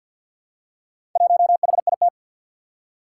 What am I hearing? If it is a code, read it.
Morse code, heard as a keyed tone, about 52 wpm, 700 Hz. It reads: 1HIT
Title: Four Challenging Letter-Number Combinations 50wpm